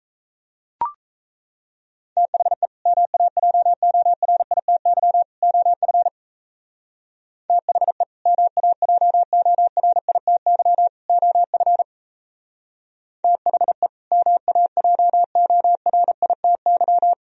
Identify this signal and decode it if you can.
Morse code, heard as a keyed tone, about 34 wpm, 700 Hz. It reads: THE MAJORITY OF THE MAJORITY OF THE MAJORITY
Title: Speed-Racing - Sets of 3 Words 28wpm